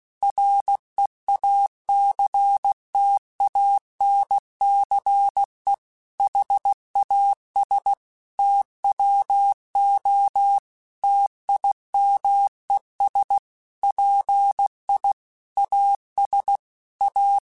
Morse code, heard as a keyed tone, about 16 wpm, 800 Hz. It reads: REACTANCE HAS TWO TIMES PI AS A
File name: Technical Subjects / Technical Subjects-mp3s / XL @15